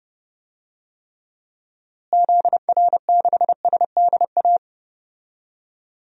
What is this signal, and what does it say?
Morse code, heard as a keyed tone, about 32 wpm, 700 Hz. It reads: ZR6SDA
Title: DXCC-100 Call Signs - 6 Characters - Repeated 3x - 30wpm